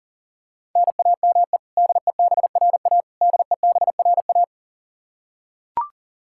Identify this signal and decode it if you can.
Morse code, heard as a keyed tone, about 40 wpm, 700 Hz. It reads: NAME DEBRA DEBRA
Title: Practice QSO - Exchange 1 - Beginning - 40wpm